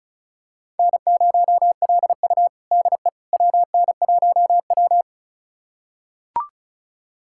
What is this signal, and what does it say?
Morse code, heard as a keyed tone, about 36 wpm, 700 Hz. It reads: N0LU DE WN1W
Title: USA Callsign DE Callsign - 4 Characters - 35wpm